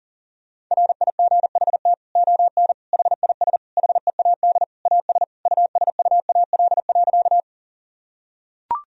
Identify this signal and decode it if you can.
Morse code, heard as a keyed tone, about 37 wpm, 700 Hz. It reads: RIGHT ON HIS HEAD AS USUAL.
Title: Sentences from Top 1000 Words - Review Part 2 - 40wpm